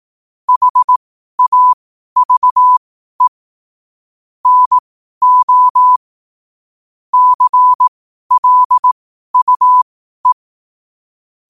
Morse code, heard as a keyed tone, about 18 wpm, spaced as for 12 wpm, 1000 Hz. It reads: HAVE NO CLUE
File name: qotdfiles / 12wpm